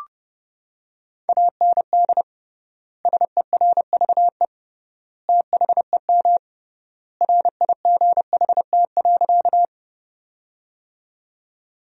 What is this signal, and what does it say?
Morse code, heard as a keyed tone, about 32 wpm, 700 Hz. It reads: AND SERVE THEM RIGHT.
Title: 3x Word-Spacing - Sentences from Top 400 Words - 30wpm